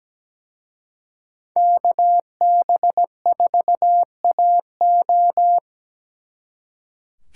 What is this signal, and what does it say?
Morse code, heard as a keyed tone, about 20 wpm, 700 Hz. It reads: KB4AO
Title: Speed Racing - USA Call Signs 17wpm